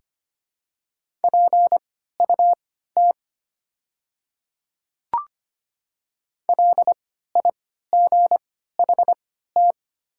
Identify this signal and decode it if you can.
Morse code, heard as a keyed tone, about 25 wpm, spaced as for 14 wpm, 700 Hz. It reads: PUT LIGHT
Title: 3x Character Spacing - Top 300 Words - 25wpm